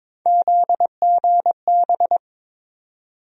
Morse code, heard as a keyed tone, about 22 wpm, 700 Hz. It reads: ZGB
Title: USA Call Sign Postfixes - 3 Letters 22wpm